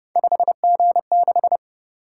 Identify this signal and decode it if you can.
Morse code, heard as a keyed tone, about 28 wpm, 700 Hz. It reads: 5G6